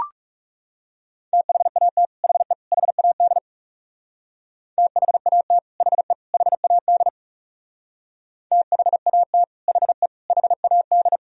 Morse code, heard as a keyed tone, about 36 wpm, 700 Hz. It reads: THAT HE HAD THAT HE HAD THAT HE HAD
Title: Speed-Racing - Sets of 3 Words 30wpm